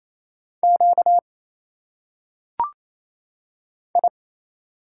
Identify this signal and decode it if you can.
Morse code, heard as a keyed tone, about 28 wpm, 700 Hz. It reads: Q I
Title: Single Letters - Phonetic Pronunciation 28wpm